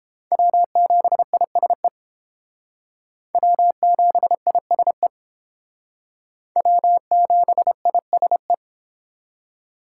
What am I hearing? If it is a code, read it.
Morse code, heard as a keyed tone, about 31 wpm, 700 Hz. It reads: W7ISE W7ISE W7ISE